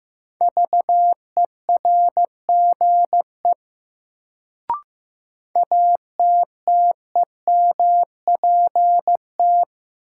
Morse code, heard as a keyed tone, about 15 wpm, 700 Hz. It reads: VERGE ATTEMPT
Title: Sets of 4 Words - New Word Drill - 15wpm